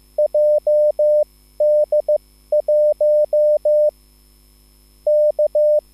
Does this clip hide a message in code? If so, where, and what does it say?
Yes: Morse code from 0.2 to 5.8 s, heard as a keyed tone, about 15 wpm, spaced as for 13 wpm, 600 Hz. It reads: JD1 K